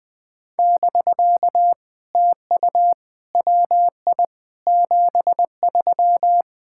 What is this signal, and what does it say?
Morse code, heard as a keyed tone, about 20 wpm, 700 Hz. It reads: <BK> TU WI 73
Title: POTA QSO Practice - 20wpm